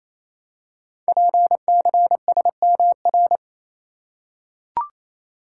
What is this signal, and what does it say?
Morse code, heard as a keyed tone, about 28 wpm, 700 Hz. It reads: PCSMR
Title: Code Groups - 5 Letter:Numbers - 28wpm